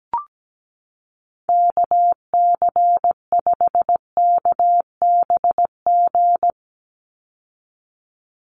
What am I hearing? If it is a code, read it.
Morse code, heard as a keyed tone, about 17 wpm, 700 Hz. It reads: KC5KBG
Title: USA Call Signs 17wpm